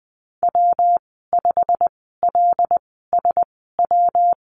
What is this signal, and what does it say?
Morse code, heard as a keyed tone, about 20 wpm, spaced as for 14 wpm, 700 Hz. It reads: W5LSW
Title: 140 Lesson 23g - Letter f - Call Signs 20wpm